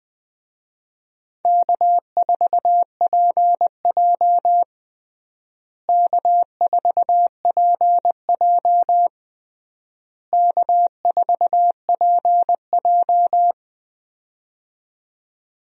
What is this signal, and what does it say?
Morse code, heard as a keyed tone, about 20 wpm, 700 Hz. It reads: K4PJ K4PJ K4PJ
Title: USA Call Signs - 4 Characters - Repeated 3x - 20wpm